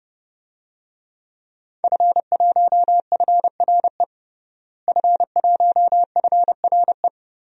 Morse code, heard as a keyed tone, about 30 wpm, 700 Hz. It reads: F1FRE F1FRE
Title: DXCC-100 Call Signs - 5 Characters - Repeated 3x - 30wpm